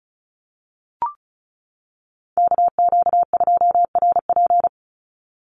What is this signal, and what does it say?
Morse code, heard as a keyed tone, about 34 wpm, 700 Hz. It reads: KQ2RP
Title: NA SOTA Chaser Callsigns 35wpm